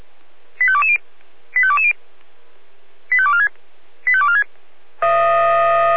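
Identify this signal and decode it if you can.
Five-tone selective call (ZVEI), heard as quick runs of five tones, tones of about 70 ms. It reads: 85220 85220 85246 85246
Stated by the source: Five-tone ZVEI selective call 85351, received on an SDR.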